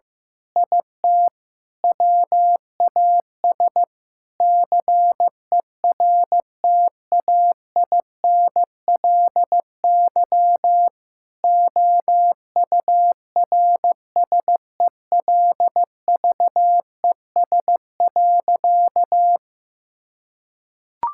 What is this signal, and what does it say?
Morse code, heard as a keyed tone, about 16 wpm, 700 Hz. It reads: IT WAS CERTAINLY OURSELVES.
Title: Sentences from Top 1600 Words 15wpm